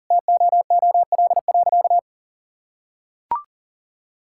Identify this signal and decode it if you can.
Morse code, heard as a keyed tone, about 40 wpm, 700 Hz. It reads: TOOL.
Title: Sentences from Top 1000 Words - Review Part 6 - 40wpm